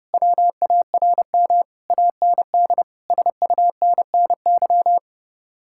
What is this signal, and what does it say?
Morse code, heard as a keyed tone, about 38 wpm, 700 Hz. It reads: WARM AND SUNNY